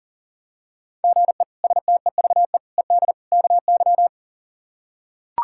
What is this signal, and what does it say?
Morse code, heard as a keyed tone, about 40 wpm, 700 Hz. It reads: GE STEVE ED KY
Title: K1USN SST QSO Practice - 40wpm